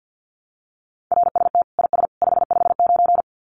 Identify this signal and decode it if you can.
Morse code, heard as a keyed tone, about 50 wpm, 700 Hz. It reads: RST IS 559
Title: Common QSO Phrases 50wpm